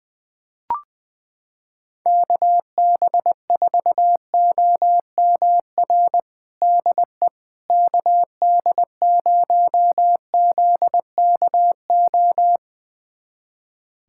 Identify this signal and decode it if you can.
Morse code, heard as a keyed tone, about 20 wpm, 700 Hz. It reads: KB4OMR DE KD0ZKO
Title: USA Callsign DE Callsign - 6 Characters - 20wpm